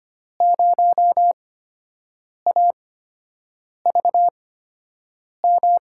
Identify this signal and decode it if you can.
Morse code, heard as a keyed tone, about 25 wpm, spaced as for 7 wpm, 700 Hz. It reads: 0AVM
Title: USA Call Signs - 8x Character Spacing - 25wpm